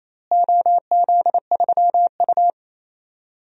Morse code, heard as a keyed tone, about 28 wpm, 700 Hz. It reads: OZ3U
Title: DXCC-100 Call Signs - 4 Characters - Repeated 3x - 28wpm